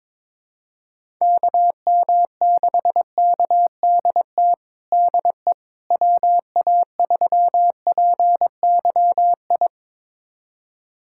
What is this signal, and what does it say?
Morse code, heard as a keyed tone, about 22 wpm, 700 Hz. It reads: KM6KDT DE WA3PYI
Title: USA Callsign DE Callsign - 6 Characters - 22wpm